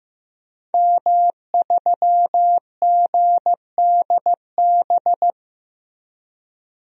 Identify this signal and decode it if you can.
Morse code, heard as a keyed tone, about 15 wpm, 700 Hz. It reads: M3GDB
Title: DXCC-100 Call Signs - 5 Characters - 15wpm